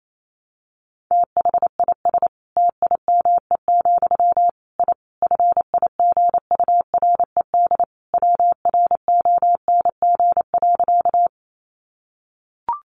Morse code, heard as a keyed tone, about 28 wpm, 700 Hz. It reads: THIS TIME, I FIGURED WRONG.
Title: A Sound Of Thunder 28wpm